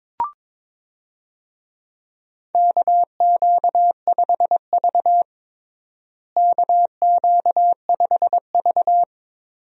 Morse code, heard as a keyed tone, about 22 wpm, 700 Hz. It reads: KQ5V KQ5V
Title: USA Call Signs - 4 Characters - Repeated 2x - 22wpm